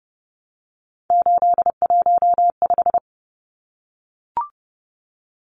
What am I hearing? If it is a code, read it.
Morse code, heard as a keyed tone, about 30 wpm, 700 Hz. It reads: 815
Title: Numbers - 3-Digit 30wpm